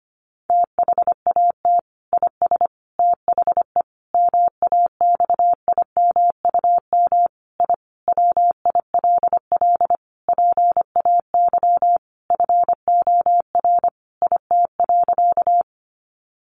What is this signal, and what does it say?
Morse code, heard as a keyed tone, about 25 wpm, 700 Hz. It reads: THAT IS THE MAXIMUM I WILL PAY FOR IT.
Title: Sentences from Top 1300 Words 25wpm